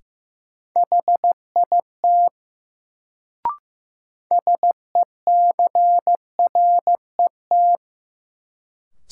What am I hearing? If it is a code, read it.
Morse code, heard as a keyed tone, about 15 wpm, 700 Hz. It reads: HIT SECRET
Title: Sentences from Top 1200 Words - New Word Drill Alt - 15wpm